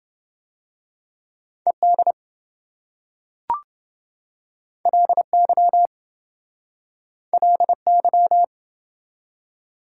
Morse code, heard as a keyed tone, about 27 wpm, 700 Hz. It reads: ED LY LY